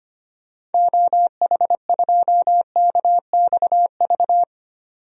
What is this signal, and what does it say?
Morse code, heard as a keyed tone, about 25 wpm, 700 Hz. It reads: OH2KXV
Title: DXCC-100 Call Signs - 6 Characters - 25wpm